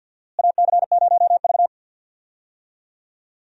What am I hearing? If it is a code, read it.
Morse code, heard as a keyed tone, about 53 wpm, 700 Hz. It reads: AC0V
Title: USA Call Signs - 4 Characters - Repeated 3x - 50wpm